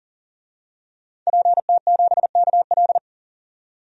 Morse code, heard as a keyed tone, about 40 wpm, 700 Hz. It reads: PT7KL
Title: DXCC-100 Call Signs - 5 Characters - 40wpm